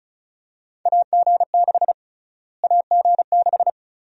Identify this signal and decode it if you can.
Morse code, heard as a keyed tone, about 36 wpm, 700 Hz. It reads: AG6 AG6
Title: USA Call Sign Prefixes - Repeated 2x - 35wpm